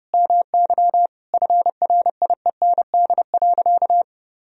Morse code, heard as a keyed tone, about 30 wpm, 700 Hz. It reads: MY FRIEND.
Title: Sentences from Top 1000 Words - Review Part 3 - 30wpm